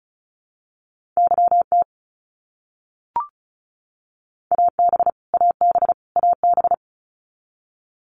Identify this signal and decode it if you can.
Morse code, heard as a keyed tone, about 36 wpm, 700 Hz. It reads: YT AB AB AB
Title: Canadian Provinces and Territories - Repeated 3x - 35wpm